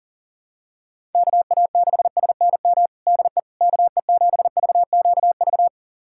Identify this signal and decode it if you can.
Morse code, heard as a keyed tone, about 41 wpm, 690 Hz. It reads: KA6SNM DE KE7VQV
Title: USA Callsign DE Callsign - 6 Characters - 40wpm